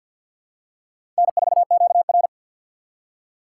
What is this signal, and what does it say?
Morse code, heard as a keyed tone, about 50 wpm, 700 Hz. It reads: N4QR
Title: USA Call Signs - 4 Characters - 3x Character Spacing - 50wpm